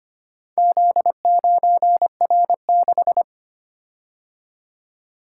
Morse code, heard as a keyed tone, about 25 wpm, 700 Hz. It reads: Z9R6
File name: practice / 25wpm / Four Letter-Number Combinations - 25wpm